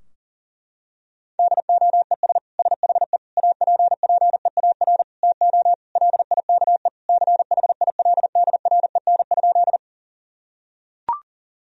Morse code, heard as a keyed tone, about 37 wpm, 700 Hz. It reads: DOES SHE APPEAR TO LIKE CHILDREN?
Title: Sentences from Top 1000 Words - Review Part 2 - 40wpm